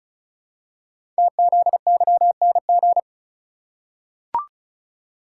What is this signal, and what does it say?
Morse code, heard as a keyed tone, about 34 wpm, 690 Hz. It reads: TZYNG
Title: Code Groups - 5 Letter:Numbers - 35wpm